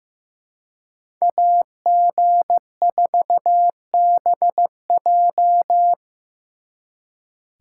Morse code, heard as a keyed tone, about 18 wpm, 700 Hz. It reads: AG4BJ
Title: Speed Racing - USA 2x2 Call Signs 15wpm